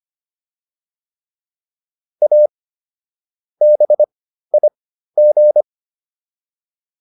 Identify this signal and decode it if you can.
Morse code, heard as a keyed tone, about 25 wpm, spaced as for 13 wpm, 600 Hz. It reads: A BIG